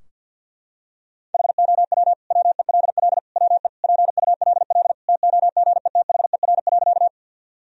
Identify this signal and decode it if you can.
Morse code, heard as a keyed tone, about 52 wpm, 700 Hz. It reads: HOW WELL WE PULL TOGETHER.
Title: Sentences from Top 1000 Words - Review Part 2 - 50wpm